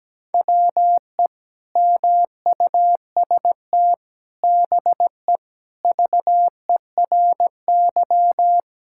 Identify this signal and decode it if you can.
Morse code, heard as a keyed tone, about 17 wpm, 700 Hz. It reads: WE MUST BE VERY
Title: Sentences from Top 1600 Words 17wpm